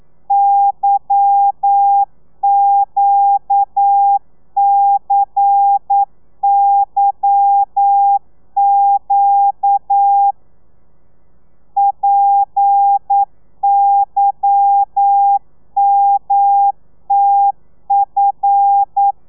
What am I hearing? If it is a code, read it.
Morse code, heard as a keyed tone, about 9 wpm, 800 Hz. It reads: YQCYQ PYMTF